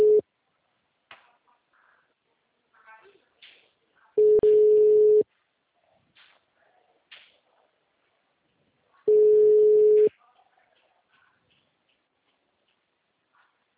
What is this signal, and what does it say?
Ringback tone (Europe, 425 Hz), with speech.
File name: out- -2002